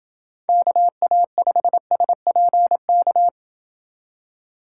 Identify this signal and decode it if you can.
Morse code, heard as a keyed tone, about 28 wpm, 700 Hz. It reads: KA5SPK